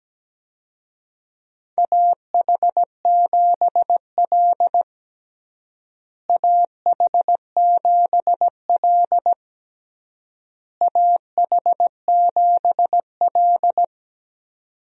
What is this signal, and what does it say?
Morse code, heard as a keyed tone, about 17 wpm, 700 Hz. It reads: AH7L AH7L AH7L
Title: USA Call Signs - 4 Characters - Repeated 3x - 17wpm